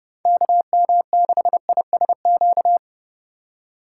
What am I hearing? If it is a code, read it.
Morse code, heard as a keyed tone, about 40 wpm, 700 Hz. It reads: KM6ISQ